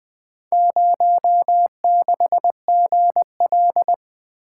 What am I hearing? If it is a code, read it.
Morse code, heard as a keyed tone, about 20 wpm, 700 Hz. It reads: 06GL
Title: Four Letter-Number Combinations - 20wpm